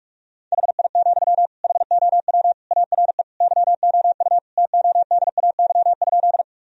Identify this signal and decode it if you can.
Morse code, heard as a keyed tone, about 43 wpm, 700 Hz. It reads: HI, HOW ARE YOU TODAY?
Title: Sentences from Top 1000 Words - Review Part 3 - 45wpm